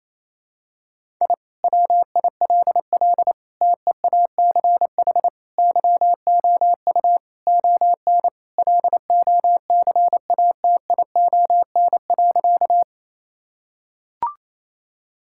Morse code, heard as a keyed tone, about 28 wpm, 700 Hz. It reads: I WILL TEACH YOU ON LOCATION.
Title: Sentences from Top 1000 Words - Review Part 6 - 28wpm